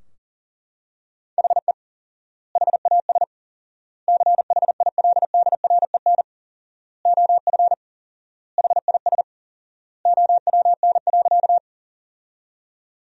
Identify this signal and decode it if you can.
Morse code, heard as a keyed tone, about 40 wpm, 700 Hz. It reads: HE HAS CHILDREN OF HIS OWN.
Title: 4x Word-Spacing - Sentences from Top 300 Words - 40wpm